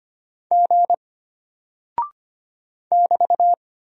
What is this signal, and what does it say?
Morse code, heard as a keyed tone, about 25 wpm, 700 Hz. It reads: G <BT>
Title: Single Character (Letter:Number:Punctuation:Prosign) Alt 25wpm